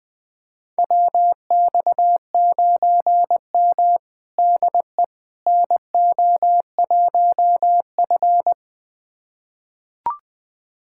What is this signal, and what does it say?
Morse code, heard as a keyed tone, about 20 wpm, 700 Hz. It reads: WX9M DE NO1F
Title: USA Callsign DE Callsign - 4 Characters - 20wpm